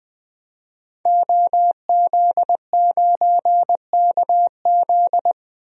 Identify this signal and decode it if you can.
Morse code, heard as a keyed tone, about 20 wpm, 700 Hz. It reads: OZ9KZ
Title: DXCC-100 Call Signs - 5 Characters - 20wpm